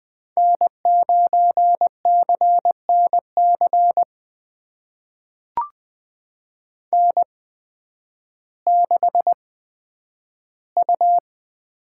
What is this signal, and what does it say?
Morse code, heard as a keyed tone, about 20 wpm, spaced as for 11 wpm, 700 Hz. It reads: N9CNC N6U
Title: USA Call Signs - 5 Characters - 8x Character Spacing - 20wpm